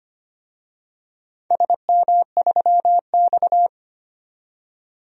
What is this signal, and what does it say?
Morse code, heard as a keyed tone, about 25 wpm, 700 Hz. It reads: SM3X
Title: DXCC-100 Call Signs - 4 Characters - 25wpm